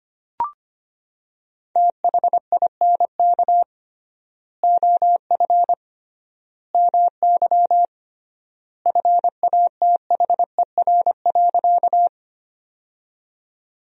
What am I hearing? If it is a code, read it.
Morse code, heard as a keyed tone, about 25 wpm, 700 Hz. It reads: THINK OF MY FATHER.
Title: 3x Word-Spacing - Sentences from Top 300 Words - 25wpm